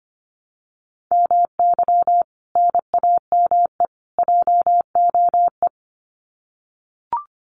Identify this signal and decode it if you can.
Morse code, heard as a keyed tone, about 25 wpm, 700 Hz. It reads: MY NAME JOE
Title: Common QSO Phrases 25wpm